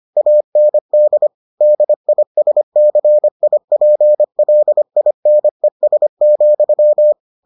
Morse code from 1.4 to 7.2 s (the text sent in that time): DISCIPLINES,